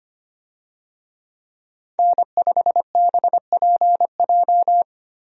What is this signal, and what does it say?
Morse code, heard as a keyed tone, about 25 wpm, 700 Hz. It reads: N5BPJ
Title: USA Call Signs - 5 Characters - 25wpm